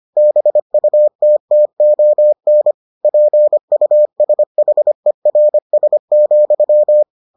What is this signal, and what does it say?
Morse code, heard as a keyed tone, about 25 wpm, 600 Hz. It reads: BUTTON PUSHERS,